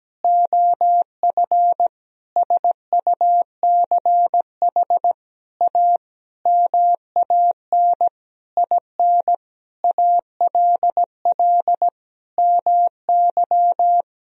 Morse code, heard as a keyed tone, about 17 wpm, 700 Hz. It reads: OF SUCH A MAN IN ALL MY
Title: Sentences from Top 1000 Words - Review Part 3 - 17wpm